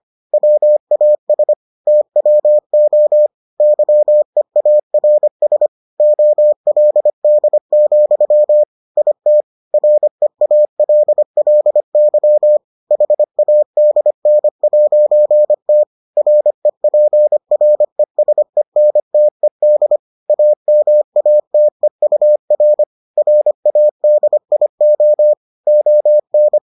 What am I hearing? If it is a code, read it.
Morse code, heard as a keyed tone, about 25 wpm, 600 Hz. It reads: WAS TWO YEARS OLD, IT REALLY HADN'T REPRESENTED AMATEUR RADIO ON